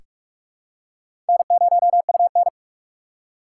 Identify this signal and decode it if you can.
Morse code, heard as a keyed tone, about 43 wpm, 700 Hz. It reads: N0UN
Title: USA Call Signs - 4x Character Spacing - 45wpm